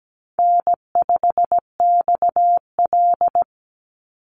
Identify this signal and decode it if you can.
Morse code, heard as a keyed tone, about 17 wpm, 700 Hz. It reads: N5XL
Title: NA SOTA Chaser Callsigns 17wpm